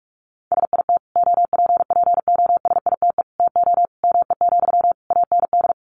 Morse code, heard as a keyed tone, about 44 wpm, 700 Hz. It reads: SIT OPPOSITE TO ME, AND
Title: Sentences from Top 1300 Words 45wpm